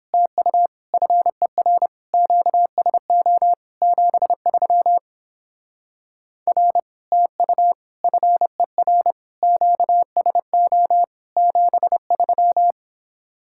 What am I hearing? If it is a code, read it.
Morse code, heard as a keyed tone, about 29 wpm, 700 Hz. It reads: TU FER QSO 73 R TU FER QSO 73